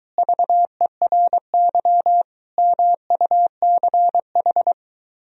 Morse code, heard as a keyed tone, about 23 wpm, 700 Hz. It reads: VERY MUCH